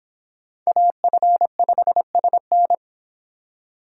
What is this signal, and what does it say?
Morse code, heard as a keyed tone, about 24 wpm, 700 Hz. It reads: AF5SN